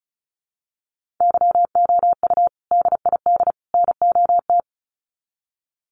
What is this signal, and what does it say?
Morse code, heard as a keyed tone, about 34 wpm, 700 Hz. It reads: YOU DID NOT
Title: Sets of 3 Words 35wpm